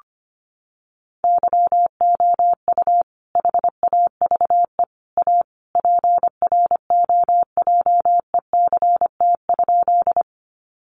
Morse code, heard as a keyed tone, about 25 wpm, 700 Hz. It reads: YOU HAVE A PROJECT?
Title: Sentences from Top 1400 Words 25wpm